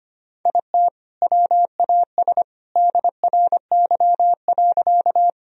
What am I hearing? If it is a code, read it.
Morse code, heard as a keyed tone, about 25 wpm, 700 Hz. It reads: IT WAS DRY.
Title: Sentences from Top 1000 Words - Review Part 3 - 25wpm